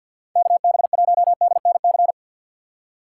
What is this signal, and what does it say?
Morse code, heard as a keyed tone, about 53 wpm, 700 Hz. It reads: KB1DNC
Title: USA Call Signs - 6 Characters - 3x Character Spacing - 50wpm